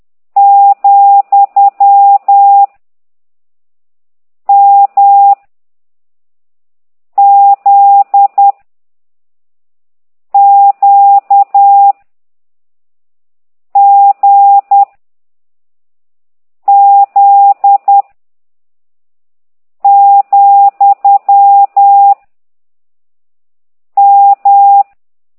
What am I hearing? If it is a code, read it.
Morse code, heard as a keyed tone, about 10 wpm, 800 Hz. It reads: , M Z Q G Z , M
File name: ct2idw / exerc11